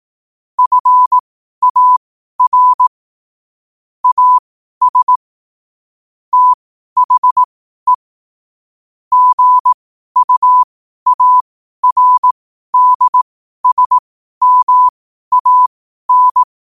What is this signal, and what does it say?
Morse code, heard as a keyed tone, about 18 wpm, spaced as for 12 wpm, 1000 Hz. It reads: FAR AS THE GUARDSMAN
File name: qotdfiles / 12wpm